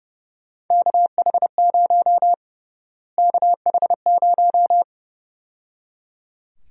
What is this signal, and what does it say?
Morse code, heard as a keyed tone, about 30 wpm, 700 Hz. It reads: KH0 KH0
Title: USA Call Sign Prefixes - Repeated 2x - 30wpm